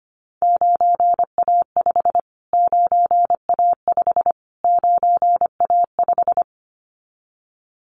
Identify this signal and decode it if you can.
Morse code, heard as a keyed tone, about 25 wpm, 700 Hz. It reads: 9A5 9A5 9A5
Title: Callsign Prefixes for Top 25 DXCC - Repeated 3x - 25wpm